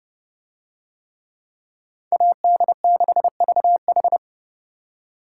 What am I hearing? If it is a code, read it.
Morse code, heard as a keyed tone, about 32 wpm, 700 Hz. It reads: AD6VH